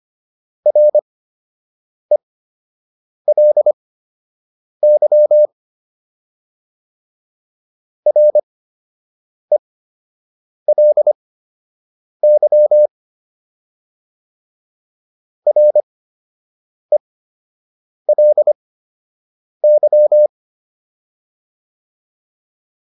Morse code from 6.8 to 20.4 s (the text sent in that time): RELY RELY